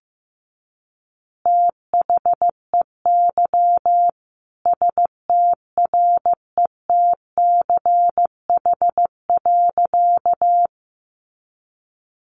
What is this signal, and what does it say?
Morse code, heard as a keyed tone, about 15 wpm, 700 Hz. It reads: THEY STRETCH.
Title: Easy Phrases 15wpm